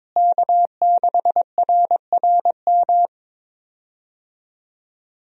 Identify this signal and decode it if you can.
Morse code, heard as a keyed tone, about 22 wpm, 700 Hz. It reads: K6RRM
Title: USA Call Signs - 5 Characters - Repeated 2x - 22wpm